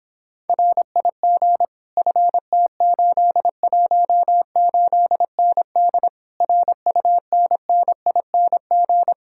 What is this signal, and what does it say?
Morse code, heard as a keyed tone, about 26 wpm, 700 Hz. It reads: RIG FT818ND RUNNING